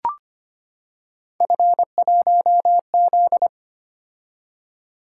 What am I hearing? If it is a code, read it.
Morse code, heard as a keyed tone, about 25 wpm, 700 Hz. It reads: F1Z
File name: Three Letter-Number Combinations - 25wpm